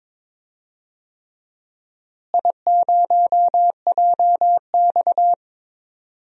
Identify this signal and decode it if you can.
Morse code, heard as a keyed tone, about 21 wpm, 700 Hz. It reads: I0JX
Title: DXCC-100 Call Signs - 4 Characters - Repeated 3x - 22wpm